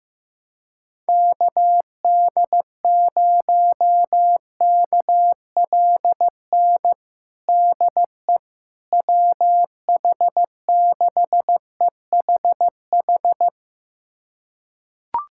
Morse code, heard as a keyed tone, about 15 wpm, 700 Hz. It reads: KD0KLN DE WH6EHH
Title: USA Callsign DE Callsign - 6 Characters - 15wpm